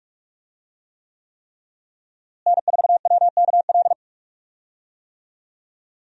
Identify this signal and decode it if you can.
Morse code, heard as a keyed tone, about 46 wpm, 700 Hz. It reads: N4WKL
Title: USA Call Signs - 5 Characters - Repeated 3x - 45wpm